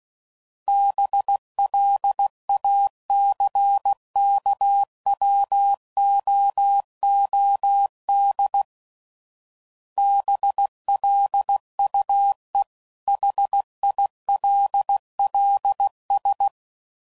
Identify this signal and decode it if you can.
Morse code, heard as a keyed tone, about 16 wpm, 800 Hz. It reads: BLACKWOOD BLUE HILLS
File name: CW Code Groups @15WPM